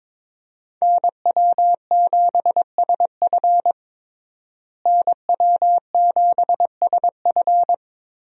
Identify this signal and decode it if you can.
Morse code, heard as a keyed tone, about 22 wpm, 700 Hz. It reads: NW7SF NW7SF
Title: USA Call Signs - 5 Characters - Repeated 3x - 22wpm